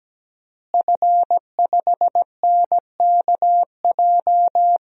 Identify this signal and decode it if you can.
Morse code, heard as a keyed tone, about 17 wpm, 700 Hz. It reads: F5NKJ
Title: DXCC-100 Call Signs - 5 Characters - Repeated 3x - 17wpm